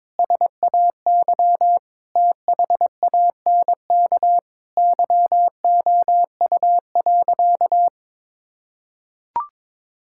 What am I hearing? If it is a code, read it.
Morse code, heard as a keyed tone, about 22 wpm, 700 Hz. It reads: SAY THANK YOU.
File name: practice / 22wpm / Sentences with 2-Word Phrases - 22wpm